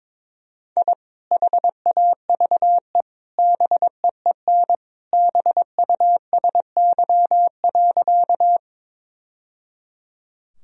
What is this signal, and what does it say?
Morse code, heard as a keyed tone, about 22 wpm, 700 Hz. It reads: I HAVE BEEN BUSY.
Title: Sentences from Top 1000 Words - Review Part 3 - 22wpm